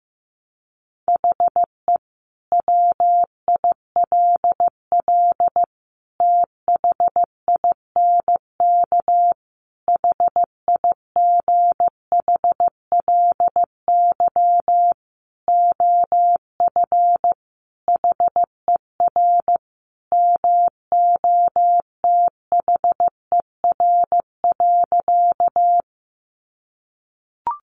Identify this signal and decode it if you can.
Morse code, heard as a keyed tone, about 15 wpm, 700 Hz. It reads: HE WILL THINK HIGHLY OF HER MOTHER.
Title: Sentences from Top 1100 Words 15wpm